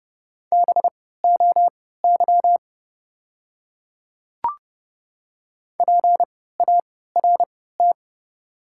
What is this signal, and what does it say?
Morse code, heard as a keyed tone, about 30 wpm, spaced as for 17 wpm, 700 Hz. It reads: BOY PART
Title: 3x Character Spacing - Top 500 Words - 30wpm